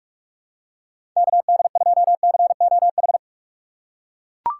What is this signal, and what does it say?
Morse code, heard as a keyed tone, about 41 wpm, 700 Hz. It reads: KD2COH
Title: USA Call Signs - 6 Characters - 2x Character Spacing - 45wpm